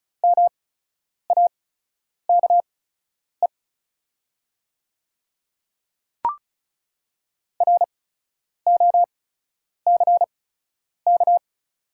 Morse code, heard as a keyed tone, about 36 wpm, spaced as for 10 wpm, 700 Hz. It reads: MAKE ROCK
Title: 8x Character Spacing - Top 500 Words - 35wpm